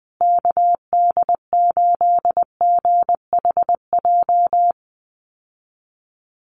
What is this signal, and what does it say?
Morse code, heard as a keyed tone, about 20 wpm, 700 Hz. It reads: KD8GHJ
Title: USA Call Signs Repeated 2x 20wpm